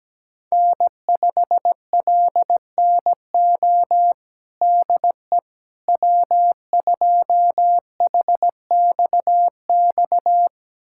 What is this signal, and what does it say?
Morse code, heard as a keyed tone, about 17 wpm, 700 Hz. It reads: N5LNO DE W2HXX
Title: USA Callsign DE Callsign - 5 Characters - 17wpm